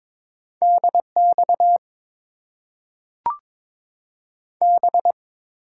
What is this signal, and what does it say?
Morse code, heard as a keyed tone, about 22 wpm, 700 Hz. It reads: DX B
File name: Top 90 QSO Elements - 8x Character Spacing - 22wpm